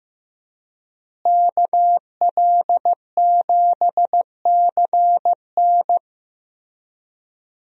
Morse code, heard as a keyed tone, about 15 wpm, 700 Hz. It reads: KL7CN
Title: USA Call Signs - 5 Characters - Repeated 2x - 15wpm